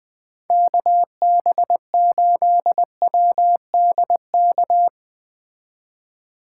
Morse code, heard as a keyed tone, about 20 wpm, 700 Hz. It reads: KB8WDK
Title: Speed Racing - USA 2x3 Call Signs 17wpm